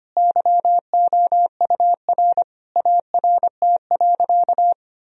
Morse code, heard as a keyed tone, about 25 wpm, 700 Hz. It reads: YOUR ART.
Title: Sentences from Top 1000 Words - Review Part 3 - 25wpm